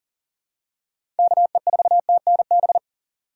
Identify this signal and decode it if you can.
Morse code, heard as a keyed tone, about 39 wpm, 700 Hz. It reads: KE4TNB
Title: USA Call Signs - 6 Characters - 6x Character Spacing - 40wpm